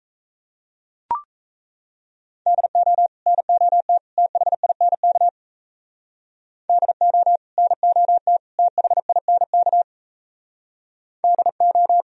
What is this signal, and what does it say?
Morse code, heard as a keyed tone, about 34 wpm, 700 Hz. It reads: DO NOT THINK DO NOT THINK DO
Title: Speed-Racing - Sets of 3 Words 28wpm